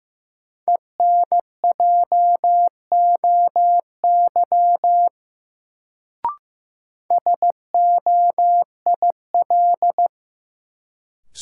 Morse code, heard as a keyed tone, about 15 wpm, 700 Hz. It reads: ENJOY SOIL
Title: Sentences from Top 1000 Words - Review Part 4 - New Word Drill - 15wpm